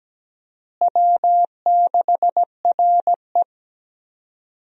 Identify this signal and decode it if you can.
Morse code, heard as a keyed tone, about 17 wpm, 700 Hz. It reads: W6RE
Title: USA Call Signs - 4 Characters - Repeated 3x - 17wpm